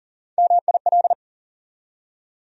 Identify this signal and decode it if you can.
Morse code, heard as a keyed tone, about 38 wpm, 700 Hz. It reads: MIL
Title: USA Call Sign Postfixes - 3 Letters 40wpm